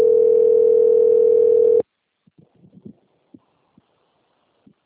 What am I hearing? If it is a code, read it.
Ringback tone (North America), 440 + 480 Hz.
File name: rg-1600-unknown- - - .4101